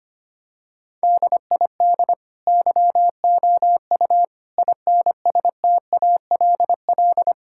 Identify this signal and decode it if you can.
Morse code, heard as a keyed tone, about 25 wpm, 700 Hz. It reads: DID YOU INSTALL
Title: Sentences from Top 1600 Words 25wpm